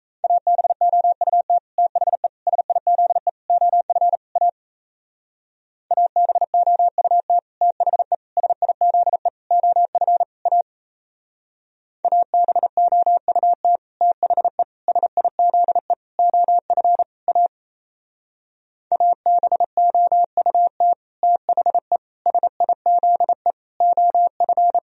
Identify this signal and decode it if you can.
Morse code, heard as a keyed tone, about 36 wpm, 700 Hz. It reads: ABOUT THE SIZE OF A ABOUT THE SIZE OF A ABOUT THE SIZE OF A ABOUT THE SIZE OF